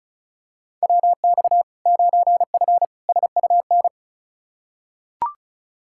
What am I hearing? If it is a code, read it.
Morse code, heard as a keyed tone, about 36 wpm, 700 Hz. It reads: WX 9F SUN
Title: Practice QSO - Exchange 2 - Info - 35wpm